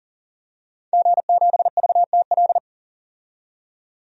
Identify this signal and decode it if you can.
Morse code, heard as a keyed tone, about 40 wpm, 700 Hz. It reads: G7VTL
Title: DXCC-100 Call Signs - 5 Characters - 40wpm